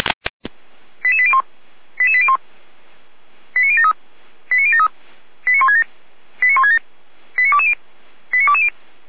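Five-tone selective call (ZVEI), heard as quick runs of five tones, tones of about 70 ms. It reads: 88801 88801 89073 89073 89167 89167 89220 89220
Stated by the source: ZVEI five-tone selective call received on SDR.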